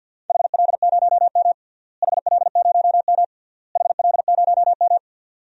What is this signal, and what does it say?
Morse code, heard as a keyed tone, about 48 wpm, 700 Hz. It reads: HL0M HL0M HL0M
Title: DXCC-100 Call Signs - 4 Characters - Repeated 3x - 50wpm